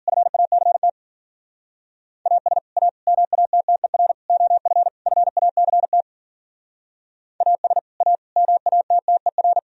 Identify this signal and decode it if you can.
Morse code, heard as a keyed tone, about 50 wpm, 700 Hz. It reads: FACT AS A MATTER OF FACT AS A MATTER
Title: Speed-Racing - Sets of 5 Words 40wpm